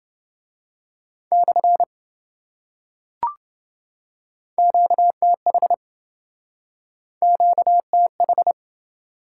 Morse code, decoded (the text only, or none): / QTH QTH